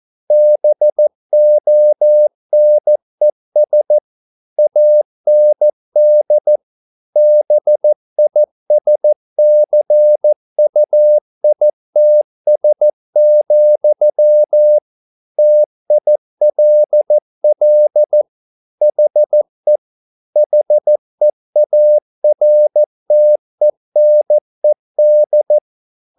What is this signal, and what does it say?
Morse code, heard as a keyed tone, about 14 wpm, 600 Hz. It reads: BONES AND BISCUITS, TILL HE HEARTENED
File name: mp3 files / a dogs mistake - 14wpm